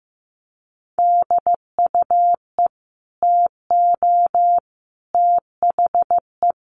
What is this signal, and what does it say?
Morse code, heard as a keyed tone, about 15 wpm, 700 Hz. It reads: DUE TO THE
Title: Sets of 5 Words Encore 15wpm